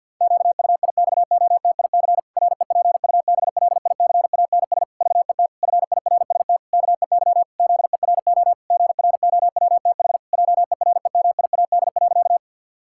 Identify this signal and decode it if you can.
Morse code, heard as a keyed tone, about 53 wpm, 700 Hz. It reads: QUIXOTIC REPUBLICANS VET FIRST KEY ZERO GROWTH JEREMIAD.
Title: Pangram Challenge v2 - 50wpm